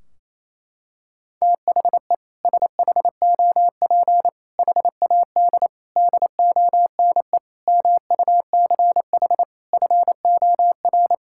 Morse code, heard as a keyed tone, about 28 wpm, 700 Hz. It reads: THE SHOP HAD DONE MUCH FOR
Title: Sentences from Top 1600 Words 28wpm